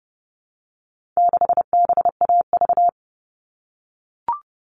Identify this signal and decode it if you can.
Morse code, heard as a keyed tone, about 30 wpm, 700 Hz. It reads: 6BAV
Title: All Cut Numbers - 4-Digit 30wpm